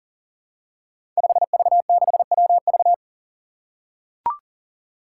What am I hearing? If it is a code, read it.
Morse code, heard as a keyed tone, about 38 wpm, 700 Hz. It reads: 5V6WV
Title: Five Challenging Letter-Number Combinations 40wpm